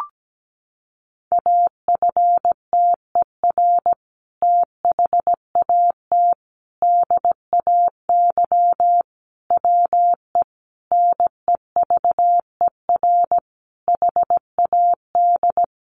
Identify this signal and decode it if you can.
Morse code, heard as a keyed tone, about 17 wpm, 700 Hz. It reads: AFTER THAT DAY WE NEVER HAD
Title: Sentences from Top 1100 Words 17wpm